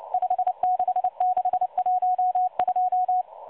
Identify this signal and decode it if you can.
Morse code, heard as a keyed tone, about 29 wpm, 730 Hz. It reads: 56612
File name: M12 - 8116 kHz CW 2-11-14